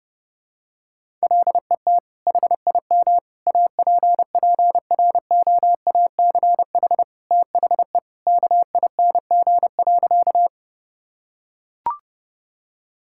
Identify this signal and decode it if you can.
Morse code, heard as a keyed tone, about 30 wpm, 700 Hz. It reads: LET HIM APPROACH THE KING.
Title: Sentences from Top 1000 Words - Review Part 6 - 30wpm